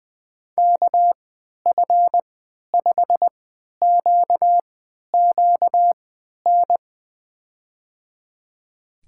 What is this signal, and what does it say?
Morse code, heard as a keyed tone, about 20 wpm, spaced as for 11 wpm, 700 Hz. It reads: KF5QQN
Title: USA Call Signs - 6 Characters - 3x Character Spacing - 20wpm